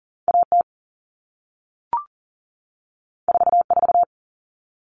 Morse code, heard as a keyed tone, about 38 wpm, 700 Hz. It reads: AT 44
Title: All Cut Numbers - 2-Digit 40wpm